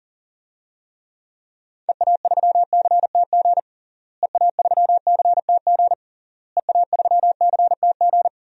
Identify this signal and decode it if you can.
Morse code, heard as a keyed tone, about 40 wpm, 700 Hz. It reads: EA3CTG EA3CTG EA3CTG
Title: DXCC-100 Call Signs - 6 Characters - Repeated 3x - 40wpm